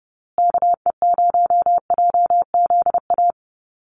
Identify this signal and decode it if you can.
Morse code, heard as a keyed tone, about 29 wpm, 700 Hz. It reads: KE0JZA
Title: USA Call Signs Repeated 2x 30wpm